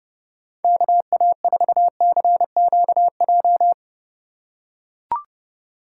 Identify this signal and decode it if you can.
Morse code, heard as a keyed tone, about 29 wpm, 700 Hz. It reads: KA4CQJ